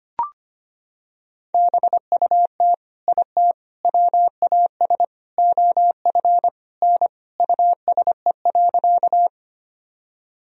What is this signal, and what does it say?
Morse code, heard as a keyed tone, about 25 wpm, 700 Hz. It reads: BUT IT WAS OF N USE.
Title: Sentences from Top 100 Words and Abbreviations 25wpm